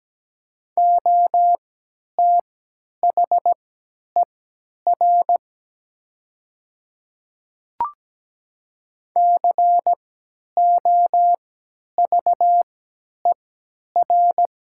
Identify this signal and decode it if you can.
Morse code, heard as a keyed tone, about 17 wpm, spaced as for 10 wpm, 700 Hz. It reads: OTHER COVER
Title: 3x Character Spacing - Top 300 Words - 17wpm